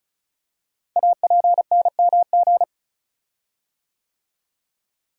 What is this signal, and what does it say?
Morse code, heard as a keyed tone, about 36 wpm, 700 Hz. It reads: APNMG
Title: Code Groups - 5 Letters - 35wpm